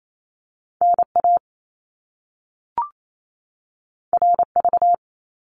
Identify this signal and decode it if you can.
Morse code, heard as a keyed tone, about 28 wpm, 700 Hz. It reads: NA RV
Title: Top Two Letter Combinations in Words 28wpm